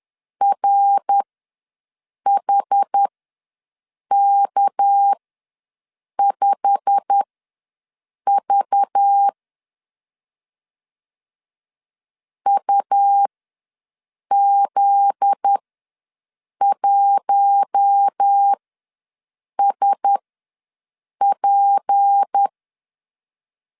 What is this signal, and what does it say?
Morse code, heard as a keyed tone, about 10 wpm, spaced as for 6 wpm, 800 Hz. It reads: RHK5V UZ1SP
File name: LEZIONI / 8 YJ45 / 8 YJ45 ws 10wpm 3843